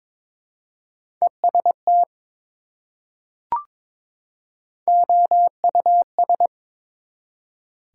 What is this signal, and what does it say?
Morse code, heard as a keyed tone, about 23 wpm, 700 Hz. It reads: EST OUS
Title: Prefixes and Suffixes 22wpm